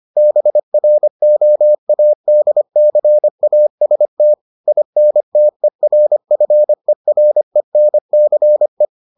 Morse code, heard as a keyed tone, about 25 wpm, 600 Hz. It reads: BROADCAST INTERFERENCE